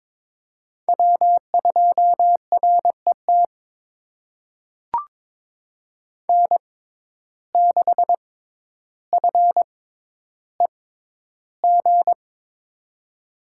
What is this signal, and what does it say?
Morse code, heard as a keyed tone, about 22 wpm, spaced as for 11 wpm, 700 Hz. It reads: W2RET N6FEG
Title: USA Call Signs - 5 Characters - 6x Character Spacing - 22wpm